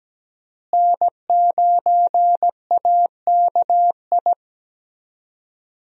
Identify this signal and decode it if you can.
Morse code, heard as a keyed tone, about 17 wpm, 700 Hz. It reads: N9AKI
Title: USA Call Signs - 5 Characters - Repeated 3x - 17wpm